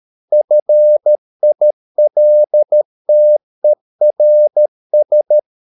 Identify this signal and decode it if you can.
Morse code, heard as a keyed tone, about 13 wpm, 600 Hz. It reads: FILTERS